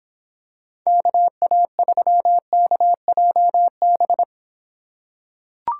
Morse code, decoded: KA3KJB

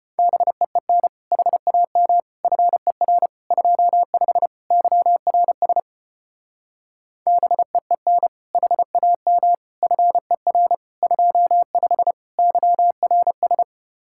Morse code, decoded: BEEN HAM FER 25 YRS BEEN HAM FER 25 YRS

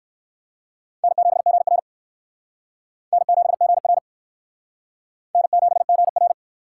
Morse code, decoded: N7GR N7GR N7GR

